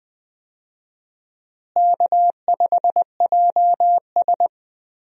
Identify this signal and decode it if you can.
Morse code, heard as a keyed tone, about 20 wpm, 700 Hz. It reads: K5JS
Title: USA Call Signs - 4 Characters - Repeated 3x - 20wpm